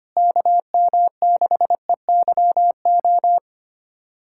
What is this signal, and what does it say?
Morse code, heard as a keyed tone, about 25 wpm, 700 Hz. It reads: KM6EYO
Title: USA Callsign DE Callsign - 6 Characters - 25wpm